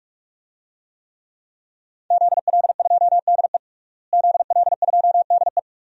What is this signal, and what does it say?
Morse code, heard as a keyed tone, about 44 wpm, 700 Hz. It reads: ZL2DE ZL2DE
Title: DXCC-100 Call Signs - 5 Characters - Repeated 3x - 45wpm